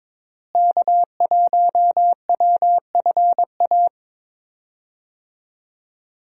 Morse code, heard as a keyed tone, about 21 wpm, 700 Hz. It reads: K1WFA
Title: USA Call Signs - 5 Characters - Repeated 2x - 22wpm